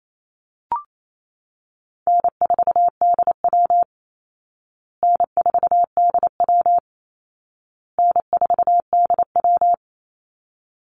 Morse code, heard as a keyed tone, about 28 wpm, 700 Hz. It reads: N4DW N4DW N4DW
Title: Easy Call Signs Repeated 3x - 28wpm